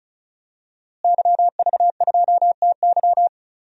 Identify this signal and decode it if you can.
Morse code, heard as a keyed tone, about 36 wpm, 700 Hz. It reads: YV2TY
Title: DXCC-100 Call Signs - 5 Characters - Repeated 3x - 35wpm